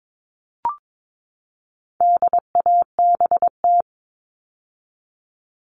Morse code, heard as a keyed tone, about 22 wpm, 700 Hz. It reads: DABT
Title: All Cut Numbers - 4-Digit 22wpm